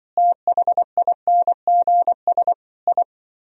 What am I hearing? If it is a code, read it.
Morse code, heard as a keyed tone, about 25 wpm, 700 Hz. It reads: THINGS I
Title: Speed-Racing - Sets of 5 Words 20wpm